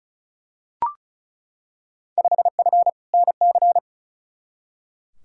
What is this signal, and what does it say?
Morse code, heard as a keyed tone, about 35 wpm, 700 Hz. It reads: 5F NC
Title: Field Day Exchanges Encore - 35wpm